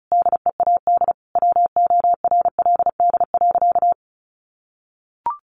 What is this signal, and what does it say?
Morse code, heard as a keyed tone, about 34 wpm, 700 Hz. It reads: DEAD WORLD.
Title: Sentences from Top 1500 Words 35wpm